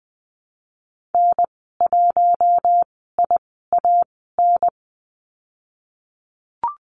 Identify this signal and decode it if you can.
Morse code, heard as a keyed tone, about 20 wpm, spaced as for 15 wpm, 700 Hz. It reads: N1IAN
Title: 076 Lesson 13g - Letter L - Call Signs 20wpm